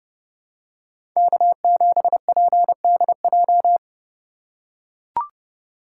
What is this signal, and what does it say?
Morse code, heard as a keyed tone, about 31 wpm, 700 Hz. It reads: K7PDJ
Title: USA Call Signs - 5 Characters - 4x Character Spacing - 30wpm